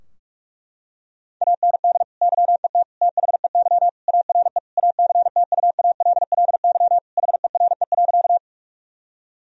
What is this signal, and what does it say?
Morse code, heard as a keyed tone, about 46 wpm, 700 Hz. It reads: AND YET THEY ARE ACTUALLY HERE.
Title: Sentences from Top 1000 Words - Review Part 4 - 45wpm